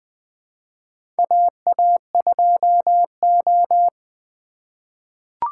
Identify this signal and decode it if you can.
Morse code, heard as a keyed tone, about 20 wpm, 700 Hz. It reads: AA2O
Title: USA Call Signs - 4 Characters - 3x Character Spacing - 20wpm